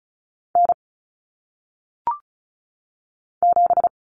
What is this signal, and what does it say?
Morse code, heard as a keyed tone, about 36 wpm, 700 Hz. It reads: N 7
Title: Common Cut Numbers - 1-Digit 35wpm